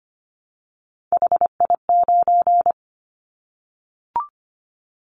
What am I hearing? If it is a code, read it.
Morse code, heard as a keyed tone, about 25 wpm, 700 Hz. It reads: HI9
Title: Callsign Prefixes for Top 100 DXCC - 25wpm